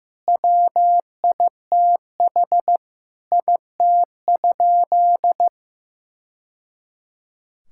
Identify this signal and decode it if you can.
Morse code, heard as a keyed tone, about 15 wpm, 700 Hz. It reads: WITH IT?
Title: Sentences from Top 1000 Words - Review Part 5 - 15wpm